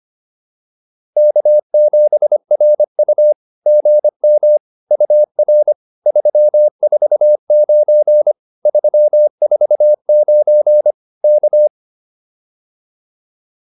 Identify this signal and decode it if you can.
Morse code, heard as a keyed tone, about 25 wpm, 590 Hz. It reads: K7RU GM UR 349 349 K